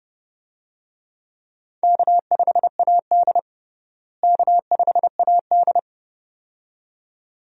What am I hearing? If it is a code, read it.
Morse code, heard as a keyed tone, about 30 wpm, 700 Hz. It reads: K5AD K5AD
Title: USA Call Signs - 4 Characters - Repeated 2x - 30wpm